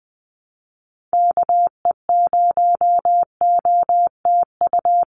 Morse code, heard as a keyed tone, about 20 wpm, 700 Hz. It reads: KE0OTU
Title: USA Call Signs 20wpm